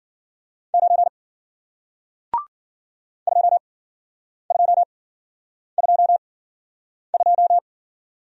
Morse code, heard as a keyed tone, about 49 wpm, spaced as for 9 wpm, 700 Hz. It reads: 9 2222
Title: Speed-Racing Single Letter-Number 40wpm